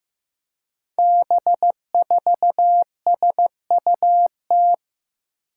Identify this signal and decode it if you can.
Morse code, heard as a keyed tone, about 15 wpm, 700 Hz. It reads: B4SUT
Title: Five Challenging Letter-Number Combinations 15wpm